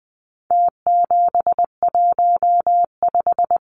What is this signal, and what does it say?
Morse code, heard as a keyed tone, about 20 wpm, 700 Hz. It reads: T715
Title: Numbers - Contest Style - 4-Digit 20wpm